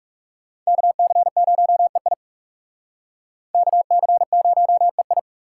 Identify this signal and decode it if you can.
Morse code, heard as a keyed tone, about 36 wpm, 700 Hz. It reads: KC0EI KC0EI
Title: Speed Racing - USA 2x2 Call Signs 30wpm